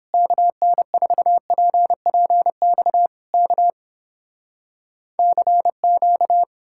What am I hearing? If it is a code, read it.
Morse code, heard as a keyed tone, about 31 wpm, 700 Hz. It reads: KN4PPX K CQ